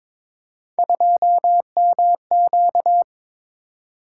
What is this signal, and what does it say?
Morse code, heard as a keyed tone, about 22 wpm, 700 Hz. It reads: 2MQ
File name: practice / 22wpm / Three Letter-Number Combinations - 22wpm